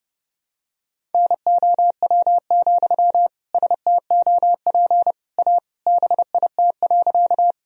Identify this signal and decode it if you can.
Morse code, heard as a keyed tone, about 32 wpm, 700 Hz. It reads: NOW, STOP A BIT.
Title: Sentences from Top 1000 Words - Review Part 1 - 30wpm